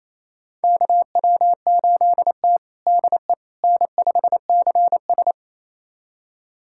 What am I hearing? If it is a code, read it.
Morse code, heard as a keyed tone, about 28 wpm, 700 Hz. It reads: KW8T DE N5CS
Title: USA Callsign DE Callsign - 4 Characters - 28wpm